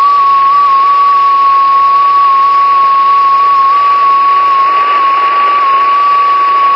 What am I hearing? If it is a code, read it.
FT8.